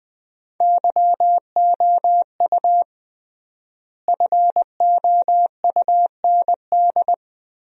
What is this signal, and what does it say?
Morse code, heard as a keyed tone, about 21 wpm, 700 Hz. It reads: YOU FOUND
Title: 3x Word-Spacing - Sentences from Top 300 Words - 20wpm